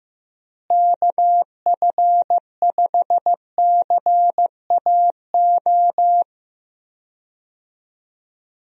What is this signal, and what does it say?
Morse code, heard as a keyed tone, about 15 wpm, 700 Hz. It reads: KF5CAO
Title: USA Call Signs - 6 Characters - Repeated 3x - 15wpm